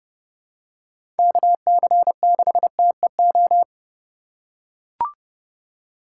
Morse code, heard as a keyed tone, about 30 wpm, 700 Hz. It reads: KC6TEO
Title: USA Call Signs - 6 Characters - 2x Character Spacing - 30wpm